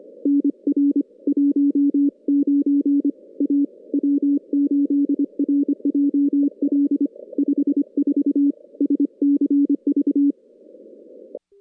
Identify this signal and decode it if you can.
Morse code, heard as a keyed tone, about 25 wpm, 340 Hz. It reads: NR 19 A W8RJL 54 SCV
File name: contests / CW-SS-k5tr- -2100